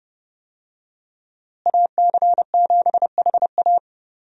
Morse code, heard as a keyed tone, about 32 wpm, 700 Hz. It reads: AC7HA
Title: USA Call Signs - 5 Characters - Repeated 2x - 30wpm